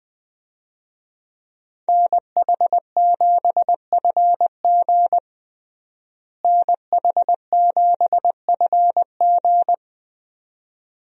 Morse code, heard as a keyed tone, about 21 wpm, 700 Hz. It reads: NH7FG NH7FG
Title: USA Call Signs - 5 Characters - Repeated 2x - 20wpm